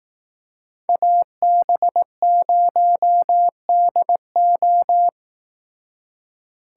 Morse code, heard as a keyed tone, about 18 wpm, 700 Hz. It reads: AB0DO
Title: Speed Racing - USA 2x2 Call Signs 15wpm